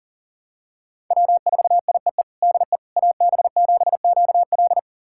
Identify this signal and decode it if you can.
Morse code, heard as a keyed tone, about 40 wpm, 700 Hz. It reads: W4IEE DE AB7QL
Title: USA Callsign DE Callsign - 5 Characters - 40wpm